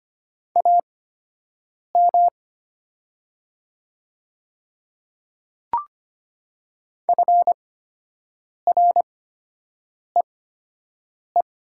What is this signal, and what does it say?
Morse code, heard as a keyed tone, about 25 wpm, spaced as for 7 wpm, 700 Hz. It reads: AM FREE
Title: 8x Character Spacing - Top 500 Words - 25wpm